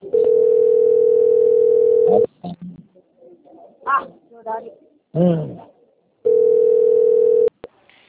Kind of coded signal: telephone call progress tone

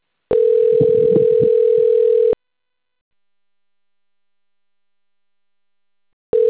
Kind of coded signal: telephone call progress tone

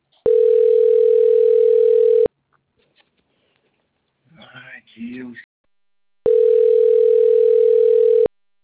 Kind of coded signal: telephone call progress tone

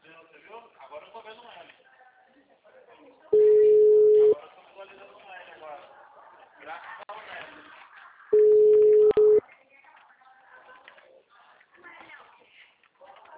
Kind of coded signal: telephone call progress tone

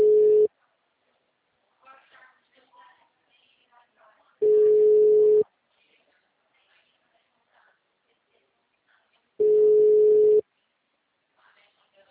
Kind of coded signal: telephone call progress tone

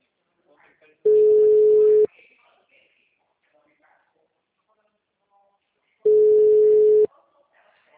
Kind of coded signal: telephone call progress tone

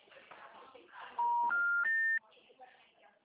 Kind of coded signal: telephone call progress tone

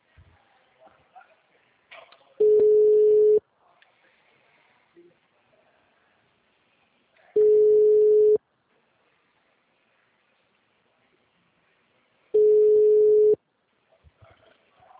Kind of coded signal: telephone call progress tone

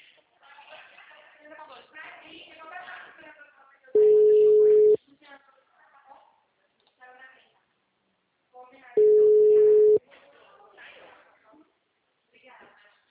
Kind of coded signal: telephone call progress tone

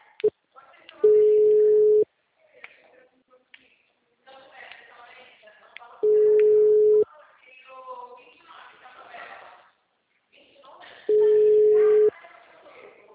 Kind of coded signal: telephone call progress tone

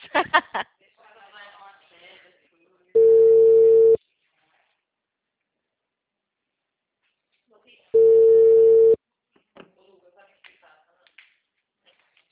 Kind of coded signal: telephone call progress tone

